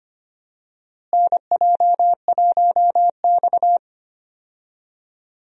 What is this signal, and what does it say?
Morse code, heard as a keyed tone, about 25 wpm, 700 Hz. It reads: NJ1X
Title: USA Call Signs - 4 Characters - Repeated 3x - 25wpm